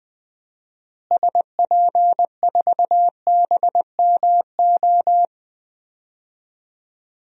Morse code, heard as a keyed tone, about 20 wpm, 700 Hz. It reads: SP4BMO
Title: DXCC-100 Call Signs - 6 Characters - Repeated 3x - 20wpm